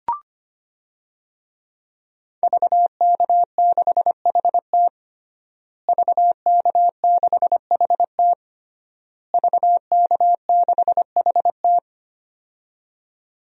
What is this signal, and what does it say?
Morse code, heard as a keyed tone, about 25 wpm, 700 Hz. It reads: VK6HT VK6HT VK6HT
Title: DXCC-100 Call Signs - 5 Characters - Repeated 3x - 25wpm